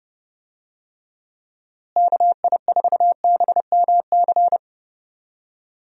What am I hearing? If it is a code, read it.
Morse code, heard as a keyed tone, about 29 wpm, 700 Hz. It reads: KI4BMC
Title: USA Call Signs - 6 Characters - Repeated 3x - 30wpm